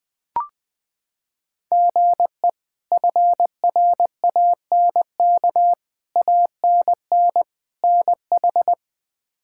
Morse code, heard as a keyed tone, about 20 wpm, 700 Hz. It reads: GE FRANK ANN NH
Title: K1USN SST QSO Practice - 20wpm